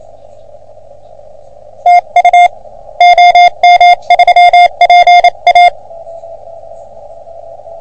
Morse code, heard as a keyed tone, about 28 wpm, 710 Hz. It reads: TU OM3PA
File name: OM3PA PILUP CUC 388